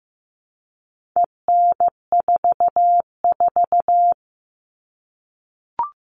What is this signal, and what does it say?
Morse code, heard as a keyed tone, about 15 wpm, 700 Hz. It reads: EN44
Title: All Cut Numbers - 4-Digit 15wpm